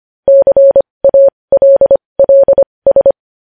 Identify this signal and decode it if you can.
Morse code, heard as a keyed tone, about 25 wpm, spaced as for 20 wpm, 550 Hz. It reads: CALLS
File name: podcast / News headlines-1x-25x20